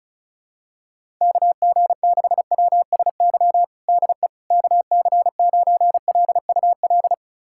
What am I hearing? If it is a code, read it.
Morse code, heard as a keyed tone, about 34 wpm, 700 Hz. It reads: KG6WSY DE KC9LUL
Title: USA Callsign DE Callsign - 6 Characters - 35wpm